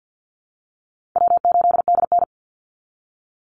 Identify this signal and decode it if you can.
Morse code, heard as a keyed tone, about 50 wpm, 700 Hz. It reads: W8DN
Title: Call Signs - 50wpm